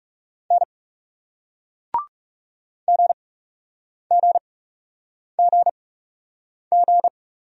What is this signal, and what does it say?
Morse code, heard as a keyed tone, about 40 wpm, spaced as for 8 wpm, 700 Hz. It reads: N GGGG